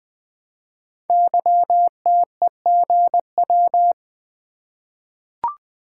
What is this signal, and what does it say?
Morse code, heard as a keyed tone, about 20 wpm, 700 Hz. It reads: YTEGW
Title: Code Groups - 5 Letter:Numbers - 20wpm